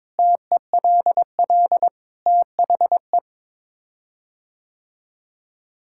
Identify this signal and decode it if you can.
Morse code, heard as a keyed tone, about 22 wpm, 700 Hz. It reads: TELL THE
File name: practice / 22wpm / Sets of 7 Words - 22wpm